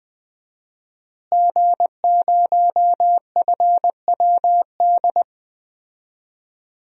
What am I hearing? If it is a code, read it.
Morse code, heard as a keyed tone, about 20 wpm, 700 Hz. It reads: G0FWD
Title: DXCC-100 Call Signs - 5 Characters - Repeated 2x - 20wpm